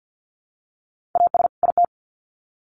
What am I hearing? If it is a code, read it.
Morse code, heard as a keyed tone, about 50 wpm, 700 Hz. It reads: AS IT
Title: Sets of 2 Words 50wpm